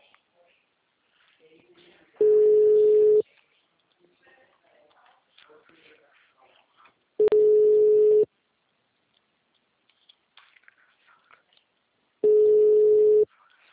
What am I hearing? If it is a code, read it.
Ringback tone (Europe, 425 Hz).